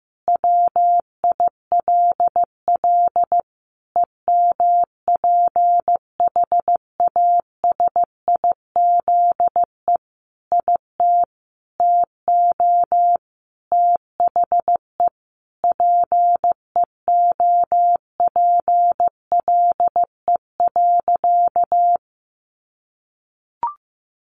Morse code, heard as a keyed tone, about 14 wpm, 700 Hz. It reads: WILL EMPHASIZE IT TO THE PEOPLE.
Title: Sentences from Top 1500 Words 15wpm